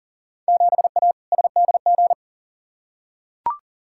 Morse code, heard as a keyed tone, about 40 wpm, 700 Hz. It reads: 7A SDG
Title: Field Day Exchanges Encore - 40wpm